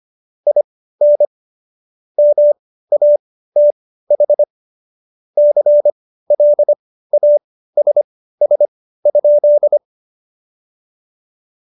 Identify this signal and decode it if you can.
Morse code, heard as a keyed tone, about 25 wpm, spaced as for 15 wpm, 590 Hz. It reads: IN MATH CLASS?
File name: Jokes-25-char-15-eff